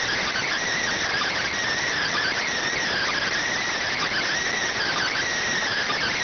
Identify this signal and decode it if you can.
Emergency Alert System (SAME) tones (header bursts), heard as harsh buzzing bursts. It is heard from 1.0 to 5.2 s.
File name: findings / CT CORONAL 710mm 10mm scout.gen-35 rx